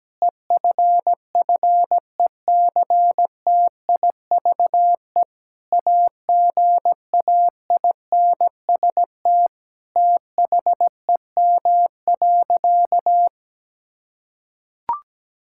Morse code, heard as a keyed tone, about 17 wpm, 700 Hz. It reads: EFFECTIVE AGAINST THEM.
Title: Sentences from Top 1000 Words - Review Part 6 - 17wpm